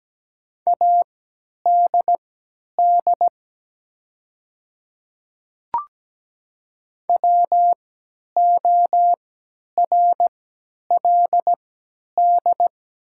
Morse code, heard as a keyed tone, about 17 wpm, spaced as for 10 wpm, 700 Hz. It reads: ADD WORLD
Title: 3x Character Spacing - Top 200 Words - 17wpm